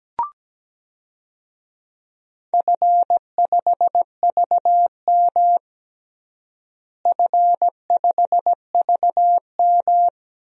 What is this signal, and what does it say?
Morse code, heard as a keyed tone, about 17 wpm, 700 Hz. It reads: F5VM F5VM
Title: DXCC-100 Call Signs - 4 Characters - Repeated 2x - 17wpm